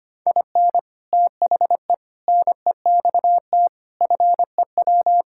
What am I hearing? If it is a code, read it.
Morse code, heard as a keyed tone, about 25 wpm, 700 Hz. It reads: IN THE NEXT FEW
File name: practice / 25wpm / Sentences with 3-Word Phrases - 25wpm